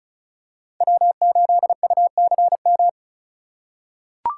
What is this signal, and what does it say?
Morse code, heard as a keyed tone, about 34 wpm, 700 Hz. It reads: W8UCM
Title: USA Call Signs - 5 Characters - 4x Character Spacing - 35wpm